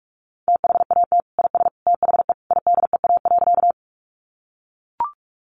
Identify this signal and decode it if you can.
Morse code, heard as a keyed tone, about 44 wpm, 700 Hz. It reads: THAT IS THE IDEA.
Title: Sentences from Top 300 Words 45wpm